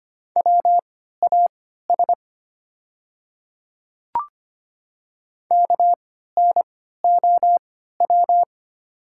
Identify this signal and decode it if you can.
Morse code, heard as a keyed tone, about 25 wpm, spaced as for 14 wpm, 700 Hz. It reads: WAS KNOW
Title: 3x Character Spacing - Top 100 Words - 25wpm